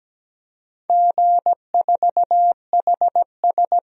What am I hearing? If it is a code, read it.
Morse code, heard as a keyed tone, about 17 wpm, 700 Hz. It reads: G4HS
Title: DXCC-100 Call Signs - 4 Characters - 17wpm